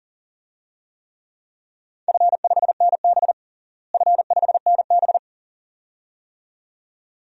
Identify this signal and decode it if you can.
Morse code, heard as a keyed tone, about 40 wpm, 700 Hz. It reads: F5NB F5NB
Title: DXCC-100 Call Signs - 4 Characters - Repeated 2x - 40wpm